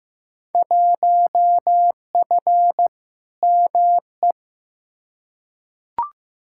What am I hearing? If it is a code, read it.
Morse code, heard as a keyed tone, about 15 wpm, 700 Hz. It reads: 1F ME
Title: Field Day Exchanges Encore - 15wpm